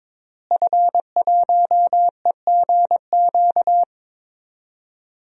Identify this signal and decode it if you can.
Morse code, heard as a keyed tone, about 22 wpm, 700 Hz. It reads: F1EGQ
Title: DXCC-100 Call Signs - 5 Characters - Repeated 2x - 22wpm